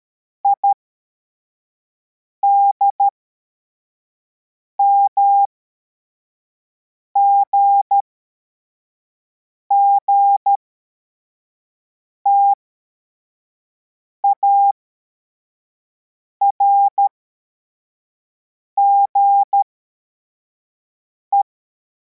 Morse code, heard as a keyed tone, about 13 wpm, spaced as for 4 wpm, 800 Hz. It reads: IDMGG TARGE
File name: formation / cw / Text34